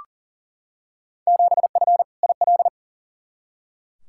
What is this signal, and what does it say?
Morse code, heard as a keyed tone, about 37 wpm, 700 Hz. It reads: 7F IL
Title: Field Day Exchanges Encore - 40wpm